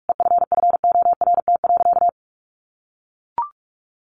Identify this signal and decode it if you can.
Morse code, heard as a keyed tone, about 44 wpm, 700 Hz. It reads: EFFORT.
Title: Sentences from Top 1500 Words 45wpm